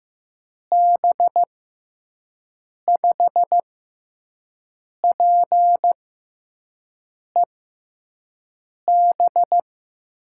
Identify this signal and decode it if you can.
Morse code, heard as a keyed tone, about 15 wpm, spaced as for 5 wpm, 700 Hz. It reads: B5PEB
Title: USA Call Signs - 6 Characters - 6x Character Spacing - 15wpm